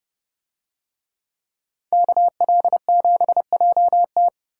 Morse code, heard as a keyed tone, about 29 wpm, 700 Hz. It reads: KL7JT